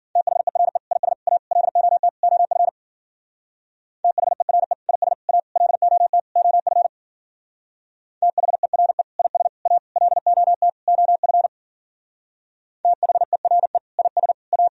Morse code, heard as a keyed tone, about 50 wpm, 700 Hz. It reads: THERE IS A LOT OF THERE IS A LOT OF THERE IS A LOT OF THERE IS A
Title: Speed-Racing - Sets of 5 Words 40wpm